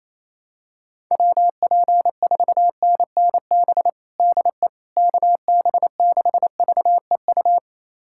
Morse code, heard as a keyed tone, about 28 wpm, 700 Hz. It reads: WP4NNB DE KB6VEU
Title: USA Callsign DE Callsign - 6 Characters - 28wpm